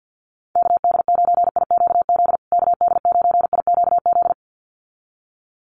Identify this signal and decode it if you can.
Morse code, heard as a keyed tone, about 48 wpm, 700 Hz. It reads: KD9IQZ KD9IQZ
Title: USA Call Signs Repeated 2x 50wpm